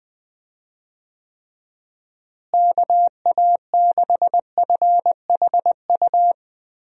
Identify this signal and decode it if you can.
Morse code, heard as a keyed tone, about 21 wpm, 700 Hz. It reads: KA6FHU
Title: USA Call Signs - 6 Characters - Repeated 3x - 20wpm